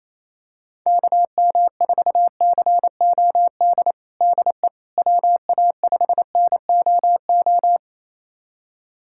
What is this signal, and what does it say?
Morse code, heard as a keyed tone, about 28 wpm, 700 Hz. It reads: KM4COD DE WA5NOO
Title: USA Callsign DE Callsign - 6 Characters - 28wpm